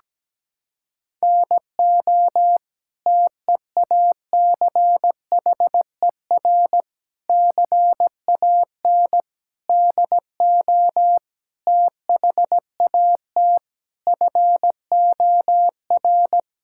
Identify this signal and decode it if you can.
Morse code, heard as a keyed tone, about 17 wpm, 700 Hz. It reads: NO TEACHER CAN DO THAT FOR
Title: Sentences from Top 1000 Words - Review Part 6 - 17wpm